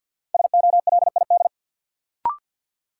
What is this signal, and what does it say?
Morse code, heard as a keyed tone, about 51 wpm, 700 Hz. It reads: SOLID
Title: Sentences from Top 1500 Words - New Word Drill - 50wpm